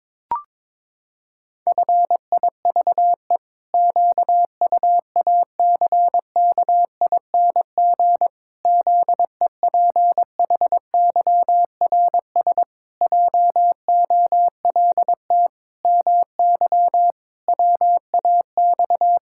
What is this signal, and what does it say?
Morse code, heard as a keyed tone, about 22 wpm, 700 Hz. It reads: FIVE QUACKING ZEPHYRS JOLT MY WAX
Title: Pangram Challenge v2 - 22wpm